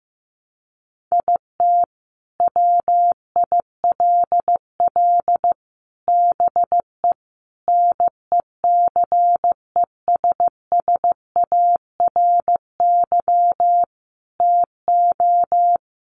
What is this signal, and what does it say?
Morse code, heard as a keyed tone, about 15 wpm, 700 Hz. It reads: IT WILL BE NECESSARY TO
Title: Sentences from Top 800 Words 15wpm